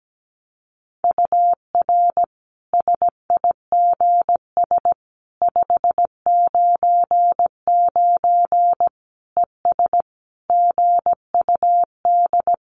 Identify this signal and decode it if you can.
Morse code, heard as a keyed tone, about 17 wpm, 700 Hz. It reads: UR SIGS 599 ES GUD
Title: Common QSO Phrases 17wpm